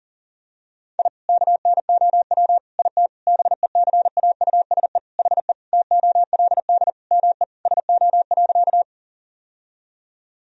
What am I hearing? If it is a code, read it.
Morse code, heard as a keyed tone, about 40 wpm, 700 Hz. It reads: I KNOW IT BECAUSE HE TOLD ME SO.
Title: Sentences from Top 1000 Words - Review Part 3 - 40wpm